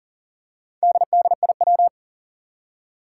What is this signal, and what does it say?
Morse code, heard as a keyed tone, about 40 wpm, 700 Hz. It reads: DDIW
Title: Four Challenging Letter-Number Combinations 40wpm